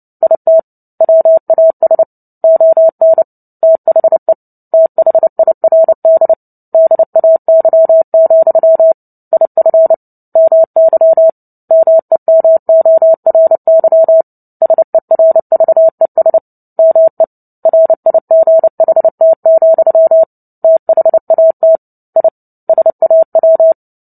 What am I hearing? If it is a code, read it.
Morse code, heard as a keyed tone, about 30 wpm, 650 Hz. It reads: IT WAS ON THE THIRD DAY, IF MY MEMORY SERVES ME RIGHT, THAT I SAW
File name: Book Two / 03-Chapter 3 - 29 WPM